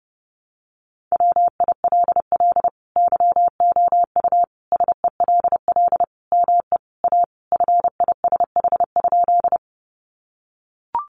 Morse code, heard as a keyed tone, about 30 wpm, 700 Hz. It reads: WILL YOU SELL ME A FISH?
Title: Sentences from Top 1100 Words 30wpm